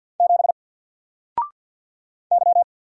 Morse code, heard as a keyed tone, about 50 wpm, 700 Hz. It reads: 7 Y
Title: Single Letter-Number 50wpm